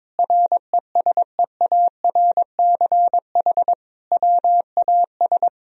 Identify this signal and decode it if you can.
Morse code, heard as a keyed tone, about 22 wpm, 700 Hz. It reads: RESEARCH WAS
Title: Sentences from Top 1000 Words - Review Part 4 - 22wpm